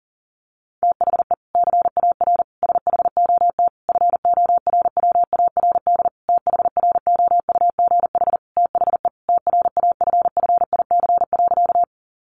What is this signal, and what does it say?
Morse code, heard as a keyed tone, about 40 wpm, 700 Hz. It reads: THE CAR SHOT FORWARD THROUGH THE TRAFFIC.
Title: Sentences from Top 1300 Words 40wpm